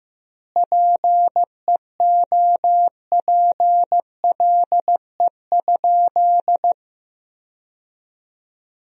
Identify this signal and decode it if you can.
Morse code, heard as a keyed tone, about 15 wpm, 700 Hz. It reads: PEOPLE?
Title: 2x Word-Spacing - Sentences from Top 200 Words - 15wpm